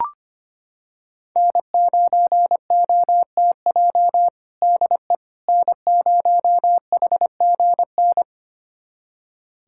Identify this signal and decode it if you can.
Morse code, heard as a keyed tone, about 25 wpm, 700 Hz. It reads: N9OTJ DE N0HGN
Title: USA Callsign DE Callsign - 5 Characters - 25wpm